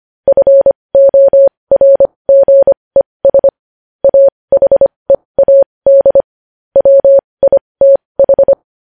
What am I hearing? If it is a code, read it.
Morse code, heard as a keyed tone, about 25 wpm, spaced as for 20 wpm, 550 Hz. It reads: FORGES AHEAD WITH
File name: podcast / News headlines-1x-25x20